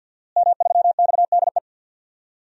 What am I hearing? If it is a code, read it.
Morse code, heard as a keyed tone, about 48 wpm, 700 Hz. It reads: M3XDE
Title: DXCC-100 Call Signs - 5 Characters - Repeated 2x - 50wpm